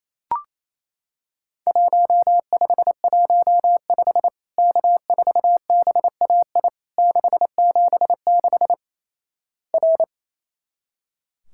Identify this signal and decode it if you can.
Morse code, heard as a keyed tone, about 28 wpm, 700 Hz. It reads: 1515 K4BAI 676 R
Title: Worked All Europe DX Contest - QTC Exchange - 28wpm